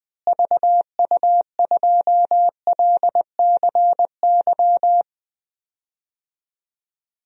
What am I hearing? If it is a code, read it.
Morse code, heard as a keyed tone, about 20 wpm, 690 Hz. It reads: VU2LCY
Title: DXCC-100 Call Signs - 6 Characters - Repeated 2x - 20wpm